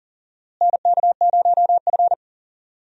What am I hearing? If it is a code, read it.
Morse code, heard as a keyed tone, about 37 wpm, 700 Hz. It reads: NK0F
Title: USA Call Signs - 4 Characters - Repeated 3x - 40wpm